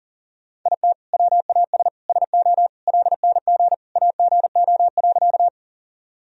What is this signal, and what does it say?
Morse code, heard as a keyed tone, about 40 wpm, 700 Hz. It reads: IT WAS SO LNG AGO.
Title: Sentences from Top 500 Words and Abbreviations 40wpm